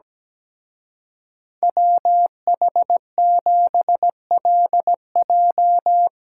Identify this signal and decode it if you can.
Morse code, heard as a keyed tone, about 19 wpm, 700 Hz. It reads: WH7LJ
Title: Speed Racing - USA 2x2 Call Signs 17wpm